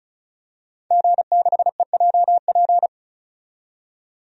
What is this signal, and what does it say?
Morse code, heard as a keyed tone, about 36 wpm, 700 Hz. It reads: G6EJP
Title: DXCC-100 Call Signs - 5 Characters - 35wpm